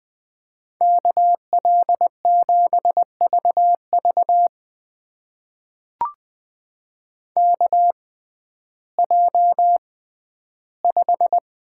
Morse code, heard as a keyed tone, about 20 wpm, spaced as for 12 wpm, 700 Hz. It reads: KL7VV KJ5
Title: USA Call Signs - 5 Characters - 6x Character Spacing - 20wpm